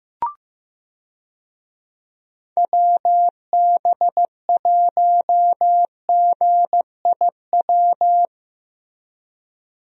Morse code, heard as a keyed tone, about 15 wpm, 700 Hz. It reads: WB1GIW
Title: USA Call Signs - 6 Characters - Repeated 3x - 15wpm